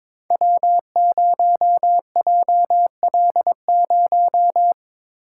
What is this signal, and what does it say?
Morse code, heard as a keyed tone, about 22 wpm, 700 Hz. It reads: W0JL0
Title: Code Groups - 5 Letter:Numbers - 22wpm